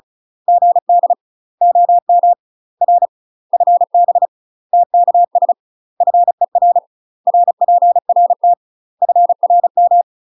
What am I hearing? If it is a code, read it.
Morse code, heard as a keyed tone, about 34 wpm, 700 Hz. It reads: GD OM R FB TKS FER RPRT FRM
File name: UK MASTER QSOs TRAINING / 35WPM / UK MASTER QSOs 0037